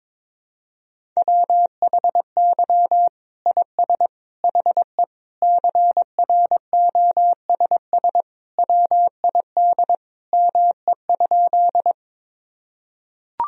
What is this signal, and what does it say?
Morse code, heard as a keyed tone, about 22 wpm, 700 Hz. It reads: WHY IS HE CROSS WID ME?
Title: Sentences from Top 300 Words and Abbreviations 22wpm